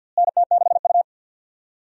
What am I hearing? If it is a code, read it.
Morse code, heard as a keyed tone, about 51 wpm, 700 Hz. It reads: NT6U
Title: USA Call Signs - 4 Characters - Repeated 2x - 50wpm